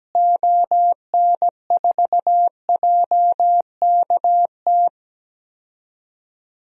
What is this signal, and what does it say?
Morse code, heard as a keyed tone, about 17 wpm, 700 Hz. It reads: ON4JKT
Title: DXCC-100 Call Signs - 6 Characters - Repeated 2x - 17wpm